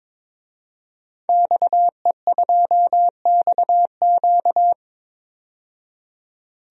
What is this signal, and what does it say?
Morse code, heard as a keyed tone, about 22 wpm, 700 Hz. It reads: XE2XQ
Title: DXCC-100 Call Signs - 5 Characters - 22wpm